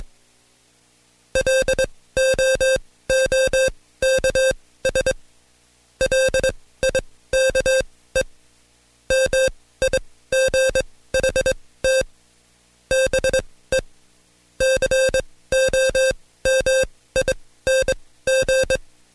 Morse code, heard as a keyed tone, about 21 wpm, spaced as for 15 wpm, 530 Hz. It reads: LOOKS LIKE MIGHT BE COMING
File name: Code Quick / AudioDownld2 / Seg134-Stepper7 15WPM